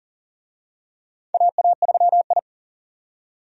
Morse code, heard as a keyed tone, about 40 wpm, 700 Hz. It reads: AA3I
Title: USA Call Signs - 4 Characters - 40wpm